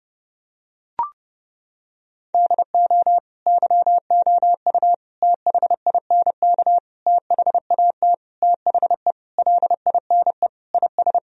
Morse code, heard as a keyed tone, about 32 wpm, 700 Hz. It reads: DO YOU THINK THAT THE LINE IS
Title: Sentences from Top 1000 Words - Review Part 1 - 30wpm